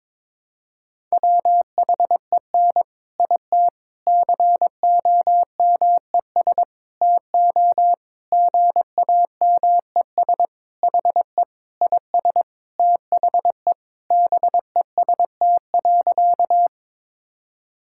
Morse code, heard as a keyed tone, about 22 wpm, 700 Hz. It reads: WHEN IT COMES TO GAMES HE IS THE BEST.
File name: practice / 22wpm / Sentences with 3-Word Phrases - 22wpm